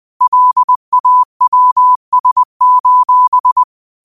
Morse code, heard as a keyed tone, about 20 wpm, 1000 Hz. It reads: LAWS: